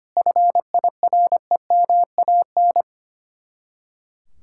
Morse code, heard as a keyed tone, about 33 wpm, 700 Hz. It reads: FIREMAN